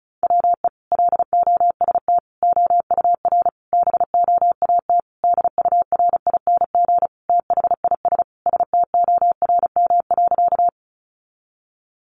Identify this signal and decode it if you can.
Morse code, heard as a keyed tone, about 34 wpm, 700 Hz. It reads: WE LOST OUR BOAT DURING THIS STORM.
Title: Sentences from Top 1500 Words 35wpm